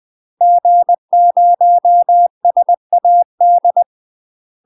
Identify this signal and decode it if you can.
Morse code, heard as a keyed tone, about 20 wpm, 700 Hz. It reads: G0SAD
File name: UK MASTER QSOs TRAINING / 20WPM / UK MASTER QSOs 0073